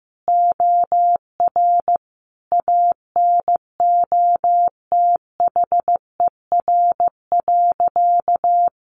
Morse code, heard as a keyed tone, about 15 wpm, 700 Hz. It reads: OR ANOTHER.
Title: Sentences from Top 1400 Words 15wpm